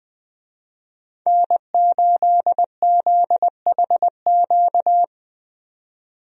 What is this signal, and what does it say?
Morse code, heard as a keyed tone, about 20 wpm, 700 Hz. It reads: N8ZHQ
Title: USA Call Signs - 5 Characters - 20wpm